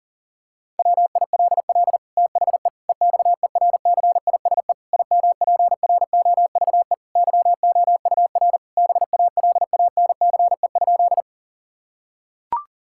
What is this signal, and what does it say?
Morse code, heard as a keyed tone, about 40 wpm, 700 Hz. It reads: WILL THE EXERCISE IMPROVE YOUR BALANCE?
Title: Sentences from Top 1000 Words - Review Part 6 - 40wpm